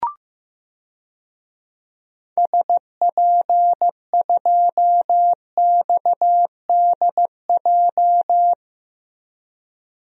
Morse code, heard as a keyed tone, about 15 wpm, 700 Hz. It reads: SP2XDJ
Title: DXCC-100 Call Signs - 6 Characters - 15wpm